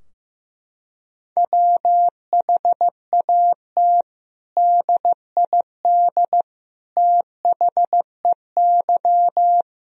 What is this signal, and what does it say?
Morse code, heard as a keyed tone, about 15 wpm, 700 Hz. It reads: WHAT DID THEY
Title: Sentences from Top 1000 Words - Review Part 3 - 15wpm